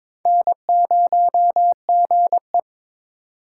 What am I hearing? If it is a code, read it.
Morse code, heard as a keyed tone, about 22 wpm, 700 Hz. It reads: N0GE
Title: USA Call Signs - 4 Characters - 4x Character Spacing - 22wpm